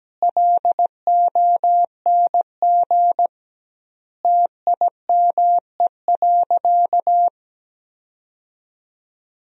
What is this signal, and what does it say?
Morse code, heard as a keyed tone, about 17 wpm, 700 Hz. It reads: LONG TIME.
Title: 2x Word-Spacing - Sentences from Top 100 Words - 17wpm